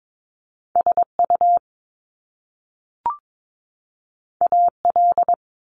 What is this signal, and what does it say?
Morse code, heard as a keyed tone, about 22 wpm, 700 Hz. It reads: SU AL
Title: Top Two Letter Combinations in Words 22wpm